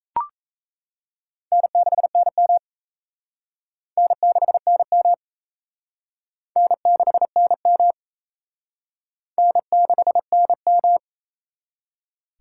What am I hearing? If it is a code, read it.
Morse code, heard as a keyed tone, about 36 wpm, 700 Hz. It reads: N6NM N6NM N6NM N6NM